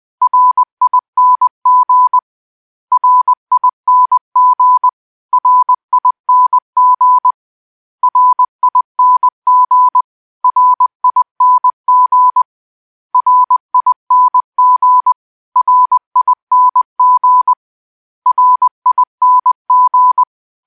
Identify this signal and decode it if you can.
Morse code, heard as a keyed tone, about 20 wpm, 1000 Hz. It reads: RING RING RING RING RING RING RING RING